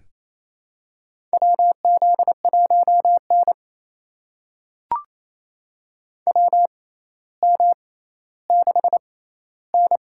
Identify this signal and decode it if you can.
Morse code, heard as a keyed tone, about 28 wpm, spaced as for 14 wpm, 700 Hz. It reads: WZ1N WM6N
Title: USA Call Signs - 4 Characters - 6x Character Spacing - 28wpm